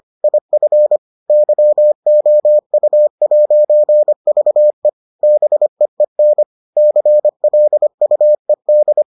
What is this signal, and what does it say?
Morse code, heard as a keyed tone, about 25 wpm, 600 Hz. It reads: IF YOU'VE BEEN CLUED